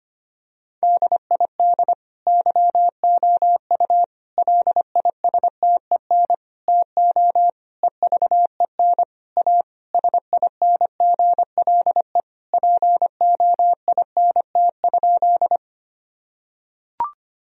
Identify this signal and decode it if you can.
Morse code, heard as a keyed tone, about 25 wpm, 700 Hz. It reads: DID YOU LISTEN TO EVEN A SINGLE POINT?
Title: Sentences from Top 1000 Words - Review Part 4 - 25wpm